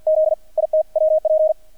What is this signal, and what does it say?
Morse code, heard as a keyed tone, about 50 wpm, 630 Hz. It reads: Z IT WP
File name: archived / CRSnet